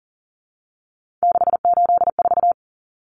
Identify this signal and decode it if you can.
Morse code, heard as a keyed tone, about 40 wpm, 700 Hz. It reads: 684
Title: Numbers - 3-Digit 40wpm